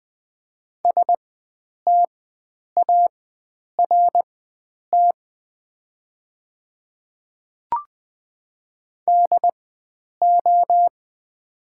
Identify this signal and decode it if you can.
Morse code, heard as a keyed tone, about 20 wpm, spaced as for 9 wpm, 700 Hz. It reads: START DO
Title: 4x Character Spacing - Top 500 Words - 20wpm